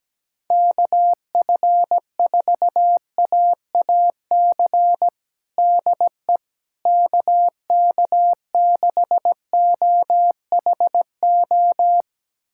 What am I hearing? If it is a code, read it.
Morse code, heard as a keyed tone, about 17 wpm, 700 Hz. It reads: KF4AAC DE KK6OHO
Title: USA Callsign DE Callsign 17wpm